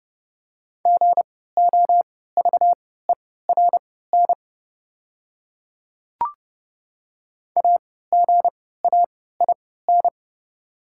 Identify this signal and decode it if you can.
Morse code, heard as a keyed tone, about 30 wpm, spaced as for 17 wpm, 700 Hz. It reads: GOVERN AGAIN
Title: 3x Character Spacing - Top 500 Words - 30wpm